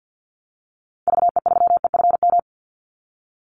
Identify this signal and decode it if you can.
Morse code, heard as a keyed tone, about 50 wpm, 700 Hz. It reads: VE3EFM
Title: Call Signs - 50wpm